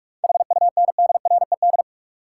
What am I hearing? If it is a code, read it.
Morse code, heard as a keyed tone, about 47 wpm, 700 Hz. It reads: HUNDRED
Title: Sentences from Top 400 Words - New Word Drill - 45wpm